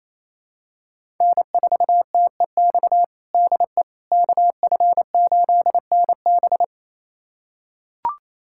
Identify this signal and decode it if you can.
Morse code, heard as a keyed tone, about 28 wpm, 700 Hz. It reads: N4TEX DE KF8NB
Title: USA Callsign DE Callsign - 5 Characters - 28wpm